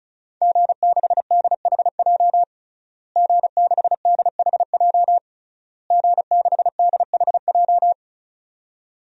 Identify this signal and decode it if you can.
Morse code, heard as a keyed tone, about 34 wpm, 700 Hz. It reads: G6DHJ G6DHJ G6DHJ
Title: DXCC-100 Call Signs - 5 Characters - Repeated 3x - 35wpm